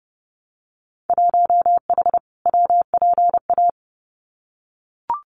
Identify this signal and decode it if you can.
Morse code, heard as a keyed tone, about 30 wpm, 700 Hz. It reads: 1H WPA
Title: Winter Field Day Exchanges 30wpm